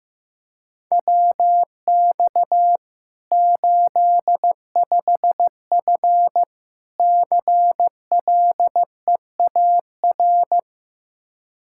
Morse code, heard as a keyed tone, about 15 wpm, 700 Hz. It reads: WX 85F CLEAR
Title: Practice QSO - Exchange 2 - Info - 15wpm